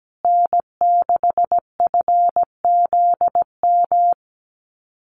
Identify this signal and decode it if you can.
Morse code, heard as a keyed tone, about 17 wpm, 700 Hz. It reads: N6FZM
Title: USA Call Signs Repeated 2x 17wpm